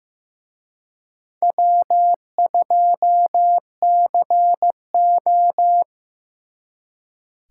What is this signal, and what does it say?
Morse code, heard as a keyed tone, about 18 wpm, 700 Hz. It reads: W2CO
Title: Speed Racing - USA 1x2 Call Signs 15wpm